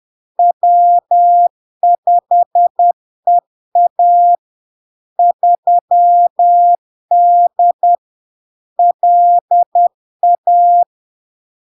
Morse code, heard as a keyed tone, about 10 wpm, 700 Hz. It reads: W5EA 3D LA
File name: LARK-FD-2022-Log-in-CW-10wpm-700-Hz